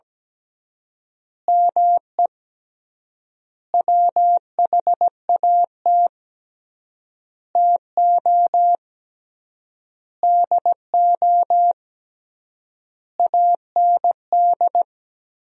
Morse code, heard as a keyed tone, about 17 wpm, 700 Hz. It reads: ME WHAT TO DO AND
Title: 3x Word-Spacing - Sentences from Top 500 Words - 17wpm